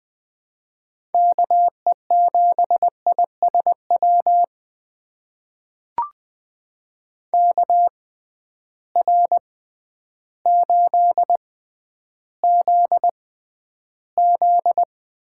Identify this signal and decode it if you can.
Morse code, heard as a keyed tone, about 20 wpm, spaced as for 11 wpm, 700 Hz. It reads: KE7ISW KR8ZZ
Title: USA Call Signs - 6 Characters - 6x Character Spacing - 20wpm